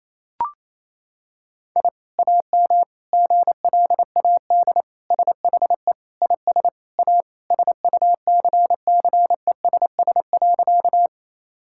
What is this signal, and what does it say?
Morse code, heard as a keyed tone, about 28 wpm, 700 Hz. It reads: I AM GLAD SHE IS A SUCCESS.
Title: Sentences from Top 1000 Words - Review Part 6 - 28wpm